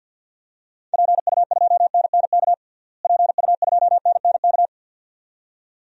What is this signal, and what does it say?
Morse code, heard as a keyed tone, about 50 wpm, 700 Hz. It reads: PU2NNK PU2NNK
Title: DXCC-100 Call Signs - 6 Characters - Repeated 2x - 50wpm